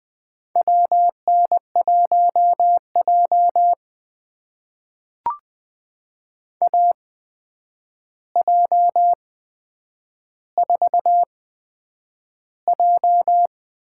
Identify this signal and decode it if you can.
Morse code, heard as a keyed tone, about 20 wpm, spaced as for 9 wpm, 700 Hz. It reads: WN1J AJ4J
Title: USA Call Signs - 4 Characters - 8x Character Spacing - 20wpm